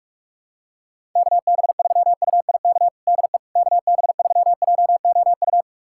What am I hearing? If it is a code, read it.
Morse code, heard as a keyed tone, about 46 wpm, 700 Hz. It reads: KB3UIK DE KB3JOU
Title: USA Callsign DE Callsign - 6 Characters - 45wpm